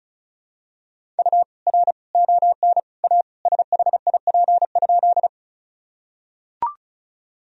Morse code, heard as a keyed tone, about 36 wpm, 700 Hz. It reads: U R ON A SHIP?
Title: Sentences from Top 400 Words and Abbreviations 35wpm